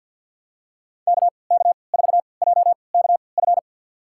Morse code, heard as a keyed tone, about 46 wpm, spaced as for 27 wpm, 700 Hz. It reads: KK4JKF
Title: USA Call Signs - 6 Characters - 3x Character Spacing - 50wpm